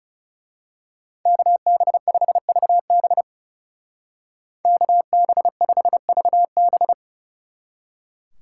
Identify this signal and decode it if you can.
Morse code, heard as a keyed tone, about 33 wpm, 700 Hz. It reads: KB5VB KB5VB
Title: Speed Racing - USA 2x2 Call Signs 30wpm